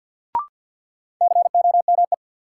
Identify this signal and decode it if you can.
Morse code, heard as a keyed tone, about 53 wpm, 700 Hz. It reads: COME
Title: Sets of 2 Words - New Word Drill - 50wpm